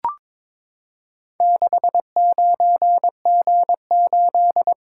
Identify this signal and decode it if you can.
Morse code, heard as a keyed tone, about 22 wpm, 700 Hz. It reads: 69G8
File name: practice / 22wpm / Four Letter-Number Combinations - 22wpm